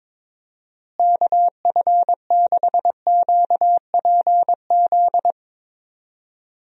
Morse code, heard as a keyed tone, about 22 wpm, 700 Hz. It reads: KF6QPZ
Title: USA Call Signs - 6 Characters - Repeated 3x - 22wpm